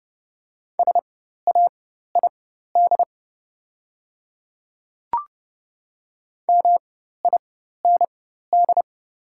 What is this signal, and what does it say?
Morse code, heard as a keyed tone, about 30 wpm, spaced as for 14 wpm, 690 Hz. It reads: SAID MIND
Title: 4x Character Spacing - Top 500 Words - 30wpm